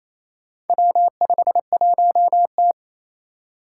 Morse code, heard as a keyed tone, about 27 wpm, 700 Hz. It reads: W51T
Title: Four Challenging Letter-Number Combinations 28wpm